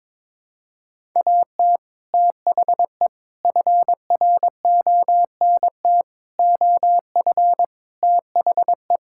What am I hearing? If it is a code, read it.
Morse code, heard as a keyed tone, about 21 wpm, 700 Hz. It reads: AT THE FRONT OF THE
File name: practice / 22wpm / Build A Phrase - 5-Word Phrases - 22wpm